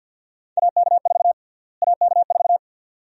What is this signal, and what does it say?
Morse code, heard as a keyed tone, about 51 wpm, 700 Hz. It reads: AK4 AK4
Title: USA Call Sign Prefixes - Repeated 2x - 50wpm